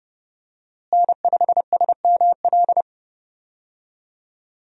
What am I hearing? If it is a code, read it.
Morse code, heard as a keyed tone, about 31 wpm, 700 Hz. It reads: N5SML
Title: USA Call Signs - 5 Characters - Repeated 2x - 30wpm